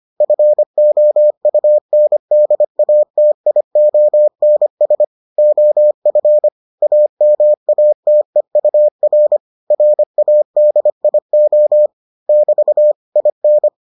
Morse code, heard as a keyed tone, about 25 wpm, 600 Hz. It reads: FOUNDATIONS OF AMATEUR RADIO <BT> IN